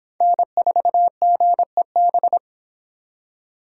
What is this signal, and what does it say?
Morse code, heard as a keyed tone, about 26 wpm, 700 Hz. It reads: N4GEB